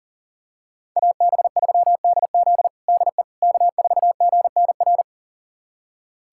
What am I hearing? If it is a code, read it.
Morse code, heard as a keyed tone, about 37 wpm, 700 Hz. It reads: AB3DZ DE K4GNR
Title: USA Callsign DE Callsign - 5 Characters - 40wpm